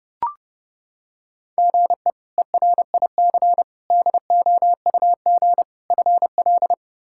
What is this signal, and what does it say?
Morse code, heard as a keyed tone, about 30 wpm, 700 Hz. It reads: GE ERIC DOUG FL
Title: K1USN SST QSO Practice - 30wpm